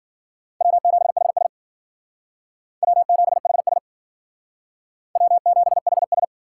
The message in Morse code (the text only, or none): W7HS W7HS W7HS